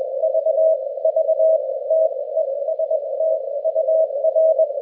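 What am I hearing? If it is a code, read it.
Morse code, heard as a keyed tone, about 21 wpm, spaced as for 18 wpm, 640 Hz. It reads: V V TEST UR
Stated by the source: QSO between DL5AXI and RD3ARU during the RU-QRP Club "Russian Hunt" (Русская Охота) contest, with mechanical key and bug use mentioned.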